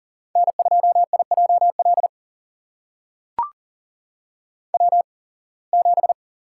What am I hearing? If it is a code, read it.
Morse code, heard as a keyed tone, about 42 wpm, spaced as for 26 wpm, 700 Hz. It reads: N2IJL W7
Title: USA Call Signs - 5 Characters - 8x Character Spacing - 40wpm